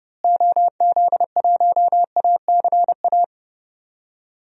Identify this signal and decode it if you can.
Morse code, heard as a keyed tone, about 30 wpm, 700 Hz. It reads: OZ1ACA
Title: DXCC-100 Call Signs - 6 Characters - Repeated 3x - 30wpm